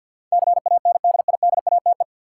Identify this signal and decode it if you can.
Morse code, heard as a keyed tone, about 52 wpm, 700 Hz. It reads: CANDIDATE
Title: Sentences from Top 1600 Words - New Word Drill Alt - 50wpm